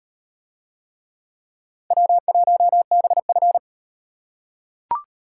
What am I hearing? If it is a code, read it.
Morse code, heard as a keyed tone, about 37 wpm, 700 Hz. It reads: W1BF